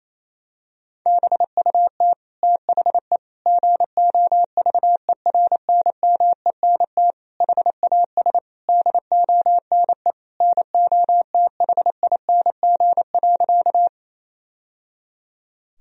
Morse code, heard as a keyed tone, about 28 wpm, 700 Hz. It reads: BUT THE GOVERNMENT HAS DONE NOTHING.
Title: Sentences from Top 1000 Words - Review Part 3 - 28wpm